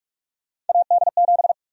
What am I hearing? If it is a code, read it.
Morse code, heard as a keyed tone, about 46 wpm, 700 Hz. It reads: AD7
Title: USA Call Sign Prefixes - 45wpm